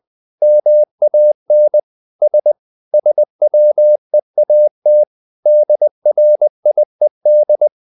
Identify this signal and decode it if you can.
Morse code, heard as a keyed tone, about 20 wpm, 600 Hz. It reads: MAN S SWEAT DRIED
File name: the old man and the sea / the old man and the sea part 1 20wpm